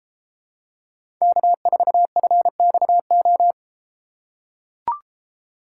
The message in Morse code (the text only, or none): K4FXO